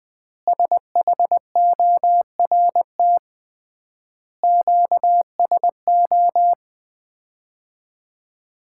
Morse code, heard as a keyed tone, about 20 wpm, 700 Hz. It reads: SHORT QSO
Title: Common QSO Phrases - 3x Word-Spacing - 20wpm